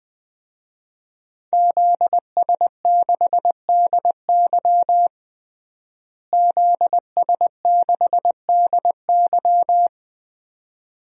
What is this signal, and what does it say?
Morse code, heard as a keyed tone, about 20 wpm, 700 Hz. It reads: ZS6DY ZS6DY
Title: DXCC-100 Call Signs - 5 Characters - Repeated 2x - 20wpm